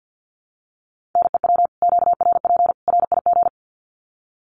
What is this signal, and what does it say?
Morse code, heard as a keyed tone, about 50 wpm, 700 Hz. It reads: NEW QRP RIG
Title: Common QSO Phrases 50wpm